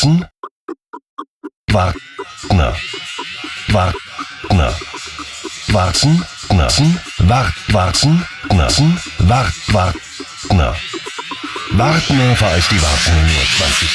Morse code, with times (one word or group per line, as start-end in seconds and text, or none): none